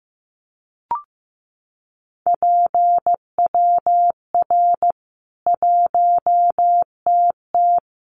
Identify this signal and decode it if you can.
Morse code, heard as a keyed tone, about 15 wpm, 700 Hz. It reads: PWR 1TT
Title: Common QSO Phrases 15wpm